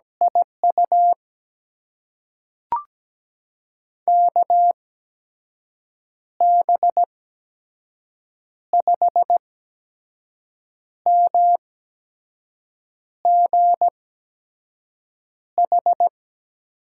Morse code, heard as a keyed tone, about 17 wpm, spaced as for 5 wpm, 700 Hz. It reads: IU KB5MGH
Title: USA Call Signs - 6 Characters - 8x Character Spacing - 17wpm